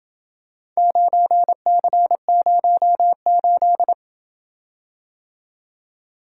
Morse code, heard as a keyed tone, about 27 wpm, 700 Hz. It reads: 9C08